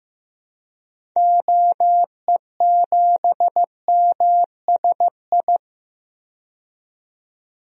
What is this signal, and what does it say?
Morse code, heard as a keyed tone, about 15 wpm, 700 Hz. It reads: OE7MSI
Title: DXCC-100 Call Signs - 6 Characters - 15wpm